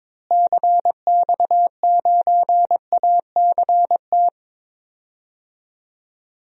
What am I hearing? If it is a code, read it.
Morse code, heard as a keyed tone, about 22 wpm, 700 Hz. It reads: CX9ACT
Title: DXCC-100 Call Signs - 6 Characters - Repeated 2x - 22wpm